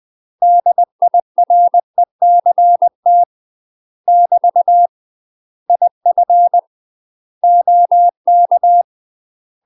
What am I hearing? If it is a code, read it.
Morse code, heard as a keyed tone, about 20 wpm, 700 Hz. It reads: DIRECT <BT> IF OK
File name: UK MASTER QSOs TRAINING / 20WPM / UK MASTER QSOs 0018